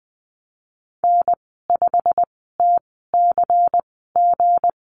Morse code, heard as a keyed tone, about 20 wpm, spaced as for 15 wpm, 700 Hz. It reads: N5TCG
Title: 186 Lesson 30g - Letter b - Call Signs 20wpm